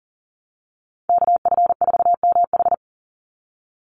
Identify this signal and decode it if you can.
Morse code, heard as a keyed tone, about 40 wpm, 700 Hz. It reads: KF4MH
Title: NA SOTA Chaser Callsigns 40wpm